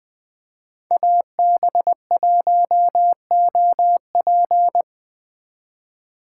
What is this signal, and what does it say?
Morse code, heard as a keyed tone, about 20 wpm, 700 Hz. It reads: AB1OP
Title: Speed Racing - USA 2x2 Call Signs 17wpm